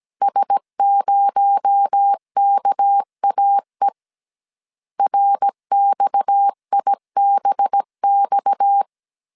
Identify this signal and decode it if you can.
Morse code, heard as a keyed tone, about 17 wpm, 800 Hz. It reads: S0KAE RXIBX
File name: 16wpm